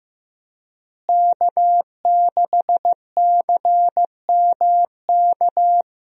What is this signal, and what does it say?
Morse code, heard as a keyed tone, about 15 wpm, 700 Hz. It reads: K6CMK
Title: USA Call Signs - 5 Characters - Repeated 3x - 15wpm